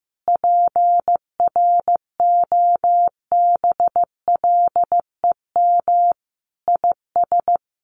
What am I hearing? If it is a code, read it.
Morse code, heard as a keyed tone, about 15 wpm, 700 Hz. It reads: PROBLEM IS
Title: Sets of 5 Words Encore2 15wpm